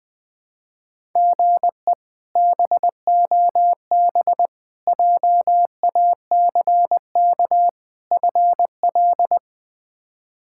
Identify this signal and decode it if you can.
Morse code, heard as a keyed tone, about 20 wpm, 700 Hz. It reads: GE BOB JACK FL
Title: K1USN SST QSO Practice - 20wpm